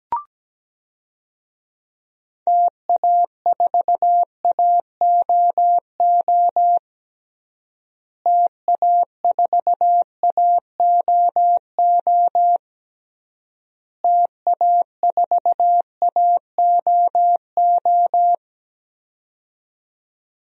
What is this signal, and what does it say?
Morse code, heard as a keyed tone, about 17 wpm, 700 Hz. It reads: TA4AOO TA4AOO TA4AOO
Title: DXCC-100 Call Signs - 6 Characters - Repeated 3x - 17wpm